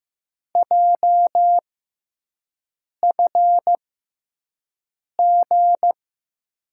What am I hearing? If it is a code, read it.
Morse code, heard as a keyed tone, about 15 wpm, spaced as for 5 wpm, 700 Hz. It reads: JFG
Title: USA Call Signs - 6 Characters - 6x Character Spacing - 15wpm